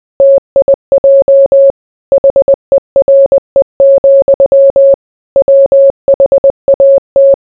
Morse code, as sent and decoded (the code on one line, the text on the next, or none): - .. .--- / .... .-. --..-- / .-- .... .- -
TIJ HERE, WHAT